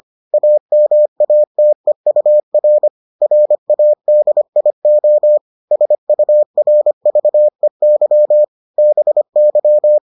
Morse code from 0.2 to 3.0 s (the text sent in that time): AMATEUR